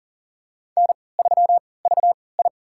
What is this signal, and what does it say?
Morse code, heard as a keyed tone, about 40 wpm, spaced as for 23 wpm, 700 Hz. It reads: N3VI
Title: USA Call Signs - 4 Characters - 3x Character Spacing - 40wpm